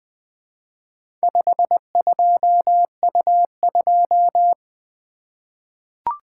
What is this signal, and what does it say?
Morse code, heard as a keyed tone, about 20 wpm, 700 Hz. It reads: 52U2
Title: Four Letter-Number Combinations - 20wpm